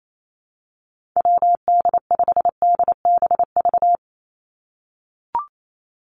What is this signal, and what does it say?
Morse code, heard as a keyed tone, about 28 wpm, 700 Hz. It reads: WD5DBV
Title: Call Signs - 28wpm